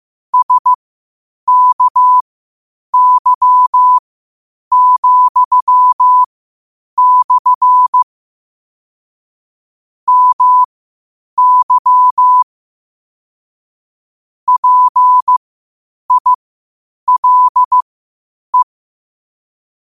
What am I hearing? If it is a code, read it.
Morse code, heard as a keyed tone, about 15 wpm, spaced as for 8 wpm, 1000 Hz. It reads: SKY,/ MY PILE
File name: qotdfiles / 8wpm